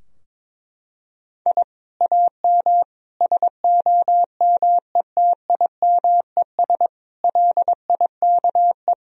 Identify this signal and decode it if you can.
Morse code, heard as a keyed tone, about 22 wpm, 700 Hz. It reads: I AM SOMETIMES LIKE
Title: Sentences from Top 1000 Words - Review Part 3 - 22wpm